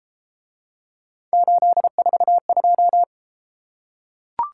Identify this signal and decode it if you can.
Morse code, heard as a keyed tone, about 31 wpm, 700 Hz. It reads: 842